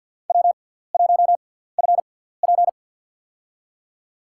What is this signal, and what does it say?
Morse code, heard as a keyed tone, about 47 wpm, spaced as for 17 wpm, 700 Hz. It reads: W1FP
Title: USA Call Signs - 4 Characters - 6x Character Spacing - 50wpm